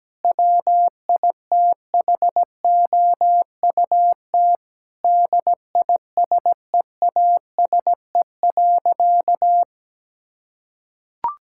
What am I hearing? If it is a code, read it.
Morse code, heard as a keyed tone, about 17 wpm, 700 Hz. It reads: WITHOUT DISEASE.
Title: Sentences from Top 1000 Words - Review Part 6 - 17wpm